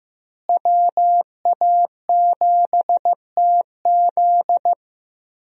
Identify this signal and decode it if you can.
Morse code, heard as a keyed tone, about 15 wpm, 700 Hz. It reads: WA7TZ
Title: USA Call Signs - 5 Characters - Repeated 2x - 15wpm